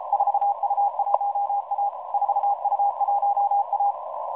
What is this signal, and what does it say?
Morse code, heard as a keyed tone, about 33 wpm, 810 Hz, with noise. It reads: 4U1A 4U1A
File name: 4u1a 14016.0kHz